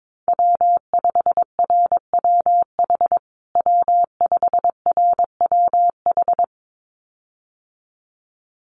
Morse code, heard as a keyed tone, about 22 wpm, 700 Hz. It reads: W5RWH W5RWH
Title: USA Call Signs Repeated 2x 22wpm